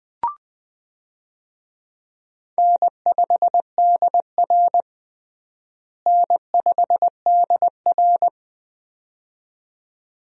Morse code, heard as a keyed tone, about 20 wpm, 700 Hz. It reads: N5DR N5DR
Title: USA Call Signs - 4 Characters - Repeated 2x - 20wpm